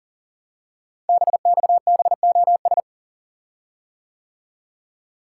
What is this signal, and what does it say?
Morse code, heard as a keyed tone, about 38 wpm, 700 Hz. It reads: BXBOS
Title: Code Groups - 5 Letters - 40wpm